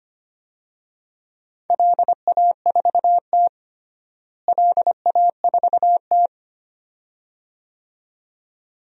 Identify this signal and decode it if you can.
Morse code, heard as a keyed tone, about 25 wpm, 700 Hz. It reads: LA4T LA4T
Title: DXCC-100 Call Signs - 4 Characters - Repeated 2x - 25wpm